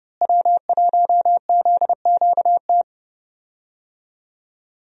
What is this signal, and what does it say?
Morse code, heard as a keyed tone, about 30 wpm, 700 Hz. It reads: W1ZQT
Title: USA Call Signs - 5 Characters - Repeated 2x - 30wpm